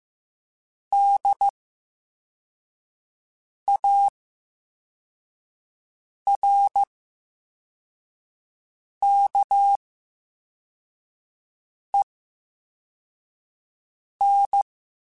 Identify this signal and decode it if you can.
Morse code, heard as a keyed tone, about 15 wpm, spaced as for 4 wpm, 800 Hz. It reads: DARKEN
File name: Special INGs / Special INGs-mp3s / Slow ing @13